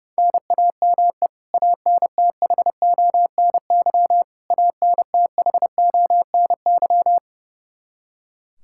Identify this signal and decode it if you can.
Morse code, heard as a keyed tone, about 32 wpm, 700 Hz. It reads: NAME ANTHONY ANTHONY
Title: Practice QSO - Exchange 1 - Beginning - 30wpm